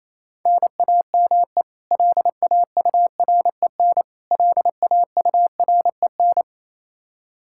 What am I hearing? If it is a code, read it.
Morse code, heard as a keyed tone, about 28 wpm, 700 Hz. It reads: NAME LAUREN LAUREN
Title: Practice QSO - Exchange 1 - Beginning - 28wpm